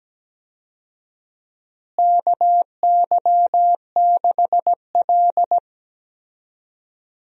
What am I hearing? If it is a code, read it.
Morse code, heard as a keyed tone, about 17 wpm, 700 Hz. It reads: KY6L
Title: USA Call Signs - 4 Characters - Repeated 2x - 17wpm